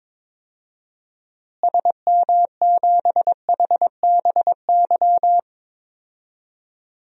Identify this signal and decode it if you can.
Morse code, heard as a keyed tone, about 22 wpm, 700 Hz. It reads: SM7HBY
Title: DXCC-100 Call Signs - 6 Characters - Repeated 3x - 22wpm